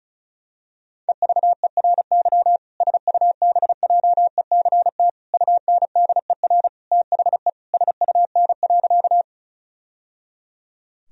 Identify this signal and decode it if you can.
Morse code, heard as a keyed tone, about 34 wpm, 700 Hz. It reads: EVERY SUBJECT UNDER THE SUN.
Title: Sentences from Top 1000 Words - Review Part 1 - 35wpm